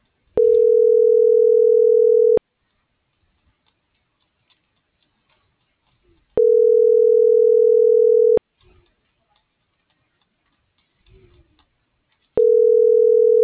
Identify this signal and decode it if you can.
Ringback tone (North America), 440 + 480 Hz.